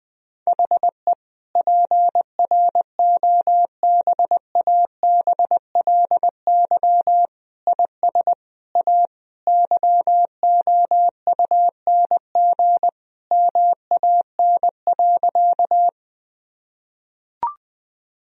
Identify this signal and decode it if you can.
Morse code, heard as a keyed tone, about 20 wpm, 700 Hz. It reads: HE PROBABLY IS A YOUNG MAN.
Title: Sentences from Top 1000 Words - Review Part 4 - 20wpm